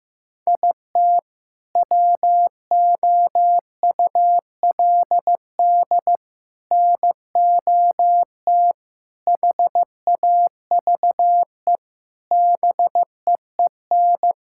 Morse code, heard as a keyed tone, about 15 wpm, 700 Hz. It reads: IT WOULD NOT HAVE BEEN
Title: Sentences from Top 1600 Words 15wpm